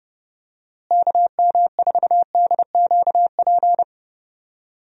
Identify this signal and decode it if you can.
Morse code, heard as a keyed tone, about 30 wpm, 700 Hz. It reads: KM4DQP
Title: USA Call Signs - 6 Characters - Repeated 3x - 30wpm